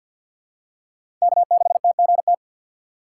Morse code, heard as a keyed tone, about 52 wpm, 700 Hz. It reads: K6TGT
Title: USA Call Signs - 2x Character Spacing - 50wpm